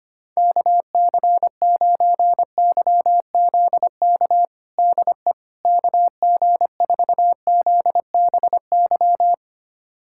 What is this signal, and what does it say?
Morse code, heard as a keyed tone, about 25 wpm, 700 Hz. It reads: KC9YZK DE KG4ZBY
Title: USA Callsign DE Callsign 25wpm